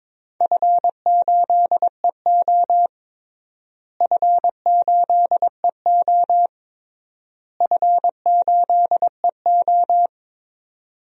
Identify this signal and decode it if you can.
Morse code, heard as a keyed tone, about 22 wpm, 700 Hz. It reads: F8EO F8EO F8EO
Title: DXCC-100 Call Signs - 4 Characters - Repeated 3x - 22wpm